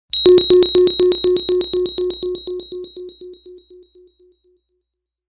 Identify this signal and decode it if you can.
There is no code.